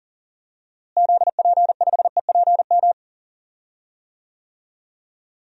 Morse code, heard as a keyed tone, about 38 wpm, 700 Hz. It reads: ZP5EPM
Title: DXCC-100 Call Signs - 6 Characters - Repeated 2x - 40wpm